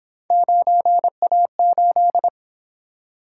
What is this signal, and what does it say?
Morse code, heard as a keyed tone, about 28 wpm, 700 Hz. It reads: 9A8